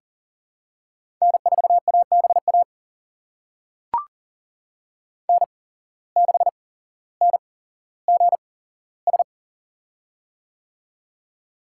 Morse code, heard as a keyed tone, about 40 wpm, spaced as for 17 wpm, 700 Hz. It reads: N4ABA N6NGS
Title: USA Call Signs - 5 Characters - 8x Character Spacing - 40wpm